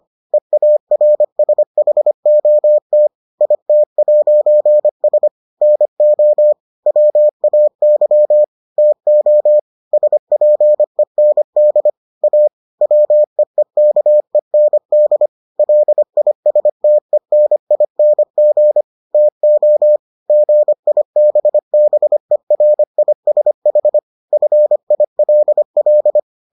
Morse code, heard as a keyed tone, about 25 wpm, 600 Hz. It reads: EARSHOT IT'S NO WAY TO SPEND A WEEKEND LISTENING TO GIBBERISH FILL